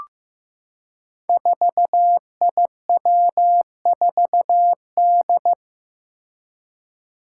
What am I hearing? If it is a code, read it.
Morse code, heard as a keyed tone, about 15 wpm, 700 Hz. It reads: 4IW4D
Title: Five Challenging Letter-Number Combinations 15wpm